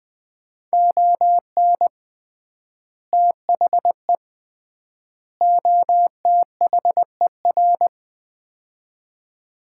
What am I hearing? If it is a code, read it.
Morse code, heard as a keyed tone, about 20 wpm, 700 Hz. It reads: ON THE OTHER
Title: 3x Word-Spacing - Sets of 3 Words - 20wpm